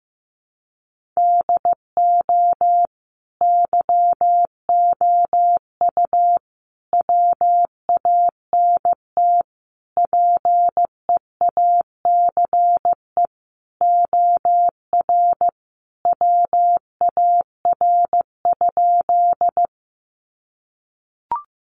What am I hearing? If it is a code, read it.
Morse code, heard as a keyed tone, about 15 wpm, 700 Hz. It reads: DO YOU WANT PEACE OR WAR?
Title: Sentences from Top 1500 Words 15wpm